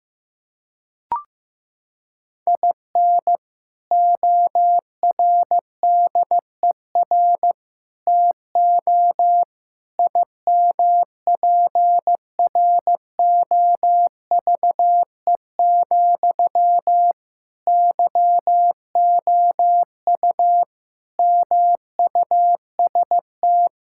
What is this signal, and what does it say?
Morse code, heard as a keyed tone, about 15 wpm, 700 Hz. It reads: IN ORDER TO IMPROVE, YOU MUST
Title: Sentences from Top 1000 Words - Review Part 5 - 15wpm